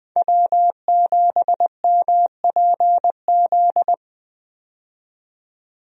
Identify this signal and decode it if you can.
Morse code, heard as a keyed tone, about 20 wpm, 700 Hz. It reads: W7MPZ
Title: USA Call Signs - 5 Characters - 20wpm